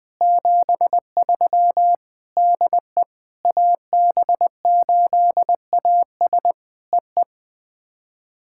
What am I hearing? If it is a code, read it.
Morse code, heard as a keyed tone, about 21 wpm, 700 Hz. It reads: 73 DE AB8AS EE
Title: POTA QSO Practice - 20wpm